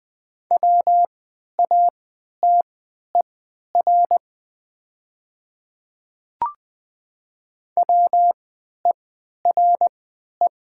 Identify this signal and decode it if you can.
Morse code, heard as a keyed tone, about 20 wpm, spaced as for 11 wpm, 700 Hz. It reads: WATER WERE
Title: 3x Character Spacing - Top 200 Words - 20wpm